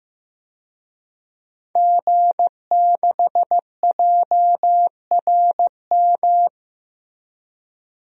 Morse code, heard as a keyed tone, about 15 wpm, 700 Hz. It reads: G6JRM
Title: DXCC-100 Call Signs - 5 Characters - 15wpm